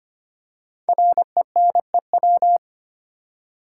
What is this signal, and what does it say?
Morse code, heard as a keyed tone, about 25 wpm, 700 Hz. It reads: RENEW
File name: 5 Letter Words - First 3 Letters Given - 25wpm